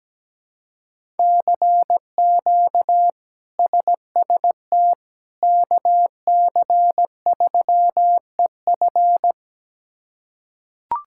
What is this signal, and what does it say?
Morse code, heard as a keyed tone, about 17 wpm, 700 Hz. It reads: CQ SST KC3EF
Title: K1USN SST QSO Practice - 17wpm